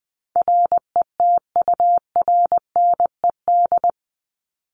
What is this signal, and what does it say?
Morse code, heard as a keyed tone, about 20 wpm, 700 Hz. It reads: RETURNED
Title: ED Suffix - Top 100 Words with Base Words 20wpm